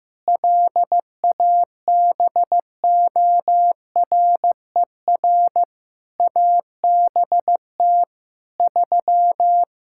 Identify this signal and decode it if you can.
Morse code, heard as a keyed tone, about 15 wpm, 700 Hz. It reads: LABORER ABT 3
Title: Practice QSO - Exchange 3 - More Info - 15wpm